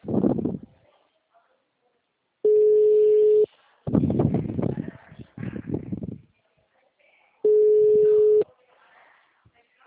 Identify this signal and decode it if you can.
Ringback tone (Europe, 425 Hz), with speech.